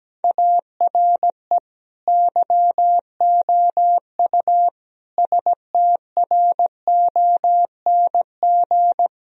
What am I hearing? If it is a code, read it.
Morse code, heard as a keyed tone, about 17 wpm, 700 Hz. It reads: ARE YOU STRONG
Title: Sentences from Top 1000 Words - Review Part 3 - 17wpm